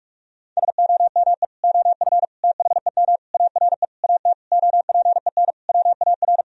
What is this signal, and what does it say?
Morse code, heard as a keyed tone, about 41 wpm, 700 Hz. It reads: SOME OF THEM ARE AT OPEN WAR
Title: Sentences from Top 1000 Words - Review Part 2 - 45wpm